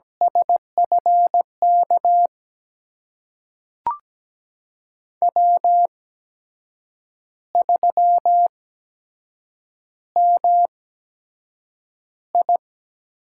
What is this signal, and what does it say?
Morse code, heard as a keyed tone, about 17 wpm, spaced as for 7 wpm, 700 Hz. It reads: SFK W3MI
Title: USA Call Signs - 8x Character Spacing - 17wpm